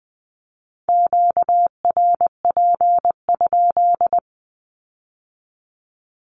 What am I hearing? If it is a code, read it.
Morse code, heard as a keyed tone, about 20 wpm, 700 Hz. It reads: QRP?
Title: Q-Codes with Spoken Meaning - 20wpm